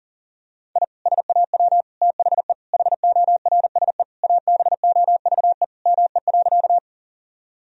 Morse code, heard as a keyed tone, about 40 wpm, 700 Hz. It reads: I SAW THE HORSE ABOVE ME.
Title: Sentences from Top 1000 Words - Review Part 2 - 40wpm